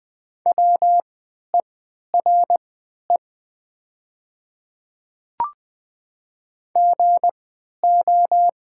Morse code, heard as a keyed tone, about 20 wpm, spaced as for 11 wpm, 700 Hz. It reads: WERE GO
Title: 3x Character Spacing - Top 100 Words - 20wpm